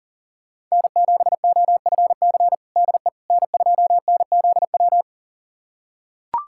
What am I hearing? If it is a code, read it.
Morse code, heard as a keyed tone, about 37 wpm, 700 Hz. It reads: N7OFC DE N2NZW
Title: USA Callsign DE Callsign - 5 Characters - 40wpm